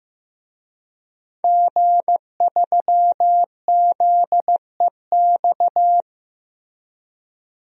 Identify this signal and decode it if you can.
Morse code, heard as a keyed tone, about 15 wpm, 700 Hz. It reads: G3ZEX
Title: DXCC-100 Call Signs - 5 Characters - 15wpm